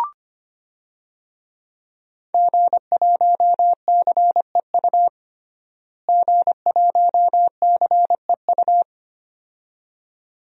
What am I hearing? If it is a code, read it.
Morse code, heard as a keyed tone, about 25 wpm, 700 Hz. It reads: G1CEU G1CEU
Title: DXCC-100 Call Signs - 5 Characters - Repeated 2x - 25wpm